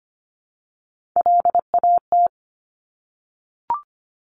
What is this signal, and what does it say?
Morse code, heard as a keyed tone, about 25 wpm, 700 Hz. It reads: LAT
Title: Top Three Letter Combinations in Words 25wpm